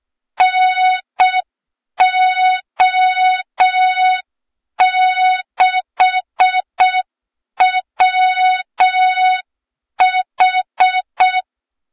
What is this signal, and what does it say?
Morse code, heard as a keyed tone, about 6 wpm, 3700 Hz. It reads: NO6WH
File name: Morze-HA2MN / hang / 23 GYAKORLAT